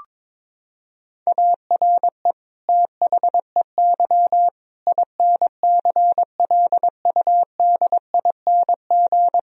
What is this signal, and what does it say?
Morse code, heard as a keyed tone, about 21 wpm, 700 Hz. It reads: ARE THEY INCLUDING
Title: Sentences from Top 1000 Words - Review Part 4 - 22wpm